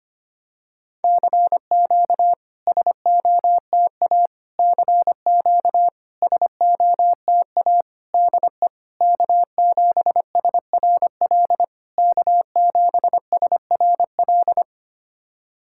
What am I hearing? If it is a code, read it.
Morse code, heard as a keyed tone, about 26 wpm, 700 Hz. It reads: CQ SOTA CQ SOTA DE K7SRL K7SRL
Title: SOTA QSO Practice 25wpm